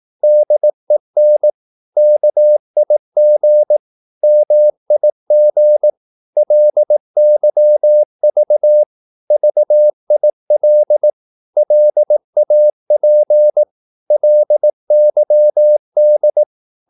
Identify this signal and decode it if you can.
Morse code, heard as a keyed tone, about 18 wpm, 600 Hz. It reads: DEN KIG MIG LYV VIL LAP LYD
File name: Korte ord DK 18wpm